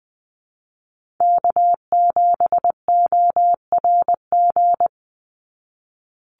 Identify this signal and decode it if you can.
Morse code, heard as a keyed tone, about 20 wpm, 700 Hz. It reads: K7ORG
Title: NA SOTA Chaser Callsigns 20wpm